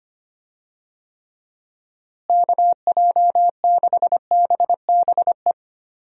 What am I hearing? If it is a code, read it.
Morse code, heard as a keyed tone, about 25 wpm, 700 Hz. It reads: KJ6BBE
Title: USA Call Signs - 6 Characters - Repeated 3x - 25wpm